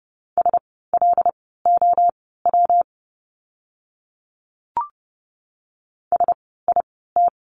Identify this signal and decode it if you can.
Morse code, heard as a keyed tone, about 30 wpm, spaced as for 17 wpm, 700 Hz. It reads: SLOW SIT
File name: course / 30:10wpm / 115 Lesson 19f - Letter w - Words 30wpm